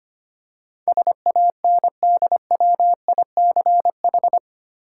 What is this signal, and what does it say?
Morse code, heard as a keyed tone, about 25 wpm, 700 Hz. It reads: SANDWICH
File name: 8 Letter Words - First 3 Letters Given - 25wpm